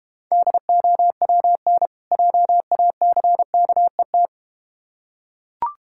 Morse code, heard as a keyed tone, about 32 wpm, 700 Hz. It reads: DOWN JACKET